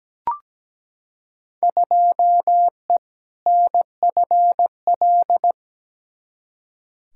Morse code, heard as a keyed tone, about 17 wpm, 700 Hz. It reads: 2E NFL
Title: Field Day Exchanges Encore - 17wpm